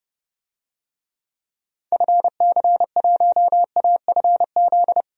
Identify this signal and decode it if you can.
Morse code, heard as a keyed tone, about 30 wpm, 700 Hz. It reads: FC1AFZ
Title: DXCC-100 Call Signs - 6 Characters - Repeated 2x - 30wpm